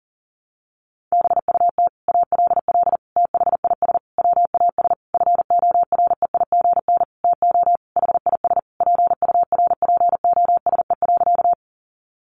Morse code, heard as a keyed tone, about 40 wpm, 700 Hz. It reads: BUT ALL THIS WAS FOREIGN TO HIS PURPOSE.
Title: Sentences from Top 1400 Words 40wpm